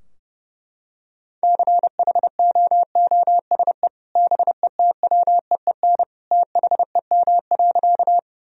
Morse code, heard as a keyed tone, about 32 wpm, 700 Hz. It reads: CHOOSE BETWEEN THEM.
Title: Sentences from Top 1000 Words - Review Part 4 - 30wpm